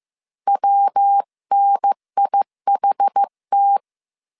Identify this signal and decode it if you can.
Morse code, heard as a keyed tone, about 15 wpm, 800 Hz. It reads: WNIHT